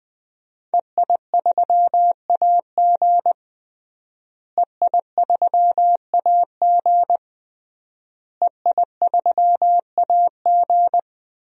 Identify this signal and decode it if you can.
Morse code, heard as a keyed tone, about 21 wpm, 700 Hz. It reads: EI3AG EI3AG EI3AG
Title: DXCC-100 Call Signs - 5 Characters - Repeated 3x - 20wpm